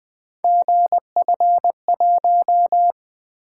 Morse code, heard as a keyed tone, about 20 wpm, 700 Hz. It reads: GF1
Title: Three Letter-Number Combinations - 20wpm